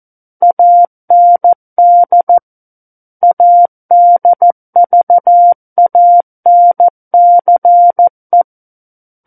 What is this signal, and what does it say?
Morse code, heard as a keyed tone, about 14 wpm, 700 Hz. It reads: AND ADVANCE
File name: Book One / 05-Chapter 5 - 14 WPM